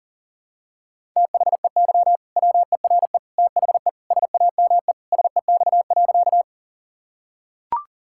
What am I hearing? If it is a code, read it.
Morse code, heard as a keyed tone, about 40 wpm, 700 Hz. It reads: THEY WERE THE SAME SEX.
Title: Sentences from Top 1600 Words 40wpm